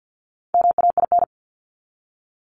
Morse code, heard as a keyed tone, about 48 wpm, 700 Hz. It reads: MAIN
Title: Top 300 Words 50wpm